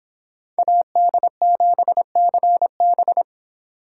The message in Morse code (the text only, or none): AD7CB